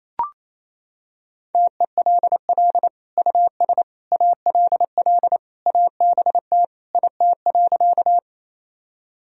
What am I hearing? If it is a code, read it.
Morse code, heard as a keyed tone, about 28 wpm, 700 Hz. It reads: TELL US ALL ABT IT.
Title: Sentences from Top 200 Words and Abbreviations 28wpm